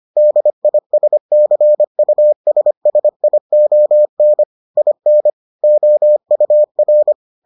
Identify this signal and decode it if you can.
Morse code, heard as a keyed tone, about 25 wpm, 600 Hz. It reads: DISCUSSION IN OUR